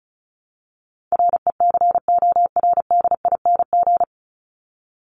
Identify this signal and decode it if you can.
Morse code, heard as a keyed tone, about 34 wpm, 700 Hz. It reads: RECORDING
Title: Top 1501-1750 Words 35wpm